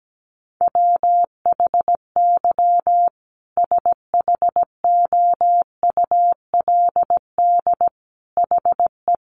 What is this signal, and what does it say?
Morse code, heard as a keyed tone, about 17 wpm, 700 Hz. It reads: WHY SHOULD HE
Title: Sentences from Top 1300 Words 17wpm